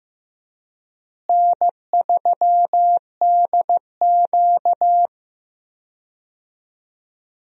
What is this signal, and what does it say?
Morse code, heard as a keyed tone, about 15 wpm, 700 Hz. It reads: N3DQ
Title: USA Call Signs - 4 Characters - Repeated 2x - 15wpm